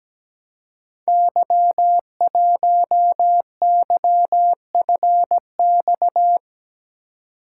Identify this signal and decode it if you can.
Morse code, heard as a keyed tone, about 17 wpm, 700 Hz. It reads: Y1YFX
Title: Code Groups - 5 Letter:Numbers - 17wpm